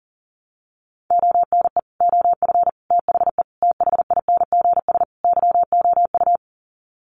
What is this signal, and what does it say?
Morse code, heard as a keyed tone, about 37 wpm, 700 Hz. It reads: ONE OF THE THINGS YOU
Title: Sets of 5 Words Encore2 40wpm